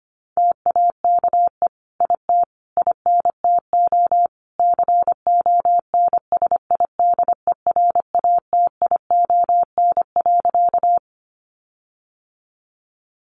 Morse code, heard as a keyed tone, about 25 wpm, 700 Hz. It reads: TAKE IT INTO CONSIDERATION.
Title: Sentences from Top 1500 Words 25wpm